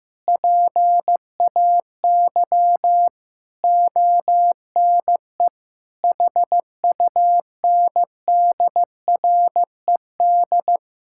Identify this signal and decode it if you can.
Morse code, heard as a keyed tone, about 15 wpm, 700 Hz. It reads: PAY ONE HUNDRED
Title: Sentences from Top 1000 Words - Review Part 6 - 15wpm